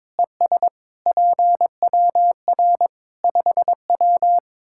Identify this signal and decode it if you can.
Morse code, heard as a keyed tone, about 21 wpm, 700 Hz. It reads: ES PWR 5W
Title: Practice QSO - Exchange 2 - Info - 22wpm